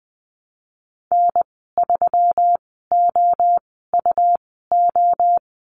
Morse code, heard as a keyed tone, about 21 wpm, spaced as for 15 wpm, 700 Hz. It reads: N3OUO
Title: 186 Lesson 30g - Letter b - Call Signs 20wpm